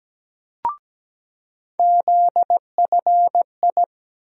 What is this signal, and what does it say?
Morse code, heard as a keyed tone, about 17 wpm, 700 Hz. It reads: ZFI
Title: USA Call Sign Postfixes - 3 Letters 17wpm